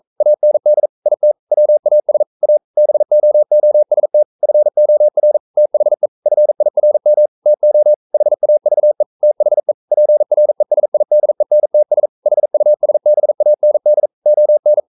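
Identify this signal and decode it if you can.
Morse code, heard as a keyed tone, about 42 wpm, 600 Hz. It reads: AND IT WAS A BOOST FOR THE FIRM TO HAVE THE PRESIDENTS HUSBAND ON
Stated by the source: Morse code practice at 41 wpm, from CW med Gnister.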